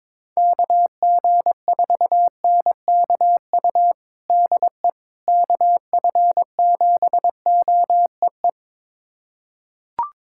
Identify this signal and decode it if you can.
Morse code, heard as a keyed tone, about 22 wpm, 700 Hz. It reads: KG4NKU DE KF7OEE
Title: USA Callsign DE Callsign - 6 Characters - 22wpm